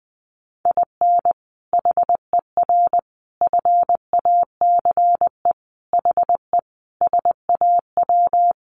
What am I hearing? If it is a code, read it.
Morse code, heard as a keyed tone, about 20 wpm, 700 Hz. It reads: IN HER FACE HE SAW
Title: Sentences from Top 1300 Words 20wpm